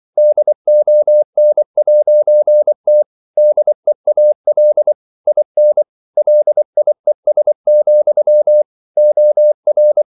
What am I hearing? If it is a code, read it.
Morse code, heard as a keyed tone, about 24 wpm, 600 Hz. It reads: DON'T DEAL IN LIES, OR
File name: CW / Practice / If24